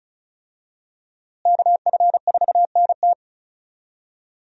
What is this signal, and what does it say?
Morse code, heard as a keyed tone, about 36 wpm, 700 Hz. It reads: KF4NT
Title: USA Call Signs - 5 Characters - Repeated 3x - 35wpm